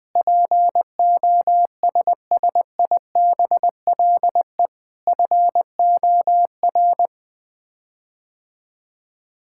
Morse code, heard as a keyed tone, about 21 wpm, 700 Hz. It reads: POSSIBLE FOR
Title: Sets of 8 Words - 20wpm